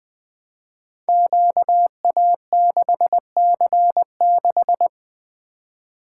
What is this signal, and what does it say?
Morse code, heard as a keyed tone, about 20 wpm, 700 Hz. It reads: QA6C6
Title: Code Groups - 5 Letter:Numbers - 20wpm